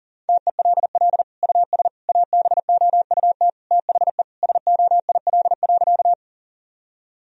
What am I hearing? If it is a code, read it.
Morse code, heard as a keyed tone, about 37 wpm, 700 Hz. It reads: TELL US ABOUT THE SOIL.
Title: Sentences from Top 1000 Words - Review Part 4 - 40wpm